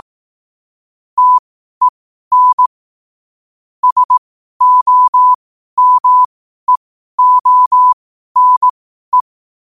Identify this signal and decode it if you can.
Morse code, heard as a keyed tone, about 18 wpm, spaced as for 12 wpm, 1000 Hz. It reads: TEN SOMEONE
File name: qotdfiles / 12wpm